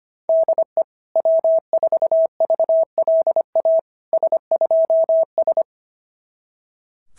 Morse code, heard as a keyed tone, about 25 wpm, 690 Hz. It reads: DE W4VLA S2S
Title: SOTA QSO Practice 25wpm